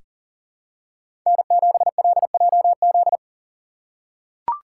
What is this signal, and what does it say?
Morse code, heard as a keyed tone, about 40 wpm, 700 Hz. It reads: N7LJZ
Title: USA Call Signs - 5 Characters - 2x Character Spacing - 40wpm